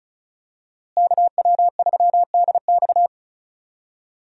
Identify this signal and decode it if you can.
Morse code, heard as a keyed tone, about 35 wpm, 700 Hz. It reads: KW3DX
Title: USA Call Signs - 5 Characters - Repeated 3x - 35wpm